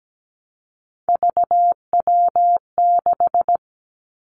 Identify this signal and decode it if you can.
Morse code, heard as a keyed tone, about 17 wpm, 700 Hz. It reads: VW6
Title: Three Challenging Letter-Number Combinations 17wpm